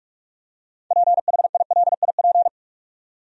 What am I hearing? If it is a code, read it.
Morse code, heard as a keyed tone, about 46 wpm, 700 Hz. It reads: PHILIP
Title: USA Common Names - 45wpm